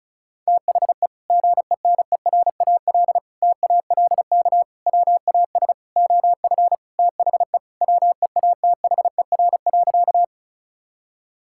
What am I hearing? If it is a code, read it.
Morse code, heard as a keyed tone, about 36 wpm, 700 Hz. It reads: THE GENERAL TALK WAS OF THE WEATHER.
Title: Sentences from Top 1000 Words - Review Part 5 - 35wpm